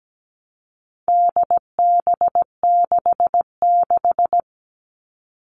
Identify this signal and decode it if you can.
Morse code, heard as a keyed tone, about 17 wpm, 700 Hz. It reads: DB66
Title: All Cut Numbers - 4-Digit 17wpm